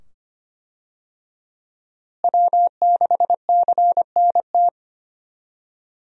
Morse code, heard as a keyed tone, about 25 wpm, 700 Hz. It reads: W6CNT
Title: USA Call Signs - 5 Characters - Repeated 2x - 25wpm